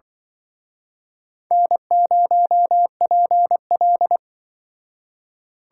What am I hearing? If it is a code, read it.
Morse code, heard as a keyed tone, about 24 wpm, 700 Hz. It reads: N0PL
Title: Speed Racing - USA 1x2 Call Signs 20wpm